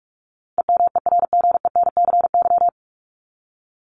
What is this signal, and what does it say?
Morse code, heard as a keyed tone, about 44 wpm, 700 Hz. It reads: EMERGENCY
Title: Top 1501-1750 Words 45wpm